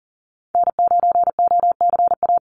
Morse code, heard as a keyed tone, about 40 wpm, 700 Hz. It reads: N9OCA
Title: USA Call Signs 40wpm